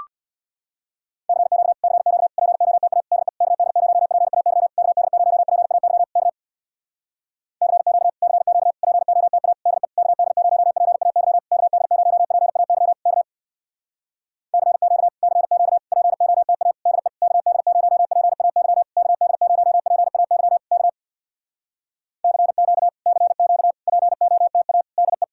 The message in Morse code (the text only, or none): CQ CQ POTA DE KD0PAY KD0PAY K CQ CQ POTA DE KD0PAY KD0PAY K CQ CQ POTA DE KD0PAY KD0PAY K CQ CQ POTA DE